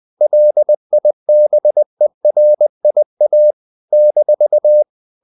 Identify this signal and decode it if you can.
Morse code, heard as a keyed tone, about 20 wpm, 600 Hz. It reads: LIBERIA -
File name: CW / Practice / NationalCapitals20